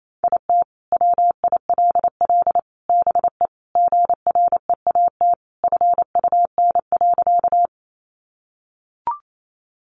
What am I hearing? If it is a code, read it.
Morse code, heard as a keyed tone, about 28 wpm, 700 Hz. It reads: IT WILL BE GREAT FUN.
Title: Sentences from Top 700 Words 28wpm